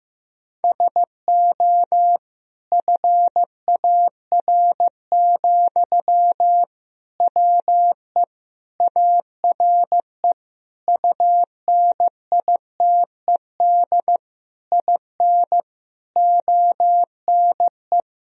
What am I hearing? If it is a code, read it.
Morse code, heard as a keyed tone, about 15 wpm, 700 Hz. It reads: SO FAR, WE ARE UNITED IN ONE
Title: Sentences from Top 1000 Words - Review Part 4 - 15wpm